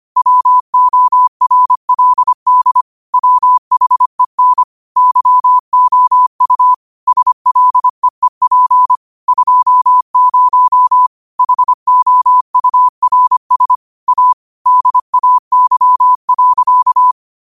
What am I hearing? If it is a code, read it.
Morse code, heard as a keyed tone, about 25 wpm, 1000 Hz. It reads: WORLD WHEN YOU SLEEP 20 HOURS A DAY.
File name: qotdfiles / 25wpm